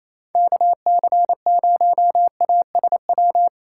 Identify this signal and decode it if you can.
Morse code, heard as a keyed tone, about 28 wpm, 700 Hz. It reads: KC0ASW
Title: USA Call Signs - 6 Characters - 2x Character Spacing - 28wpm